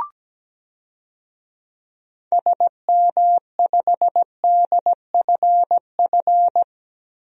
Morse code, heard as a keyed tone, about 17 wpm, 700 Hz. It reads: SM5DFF
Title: DXCC-100 Call Signs - 6 Characters - Repeated 2x - 17wpm